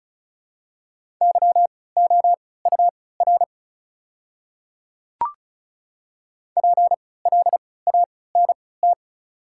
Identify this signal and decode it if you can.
Morse code, heard as a keyed tone, about 36 wpm, spaced as for 20 wpm, 690 Hz. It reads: YOUR PLANT
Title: 3x Character Spacing - Top 500 Words - 35wpm